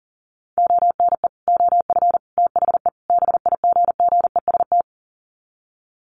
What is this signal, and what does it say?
Morse code, heard as a keyed tone, about 40 wpm, 700 Hz. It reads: ONE OF THE BIGGEST
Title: Sets of 4 Words 40wpm